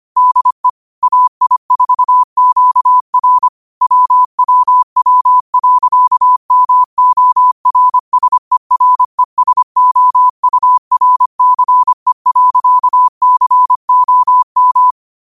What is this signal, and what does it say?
Morse code, heard as a keyed tone, about 25 wpm, 1000 Hz. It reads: DE AI4QR WWW.MORSERESOURCE.COM
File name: qotdfiles / 25wpm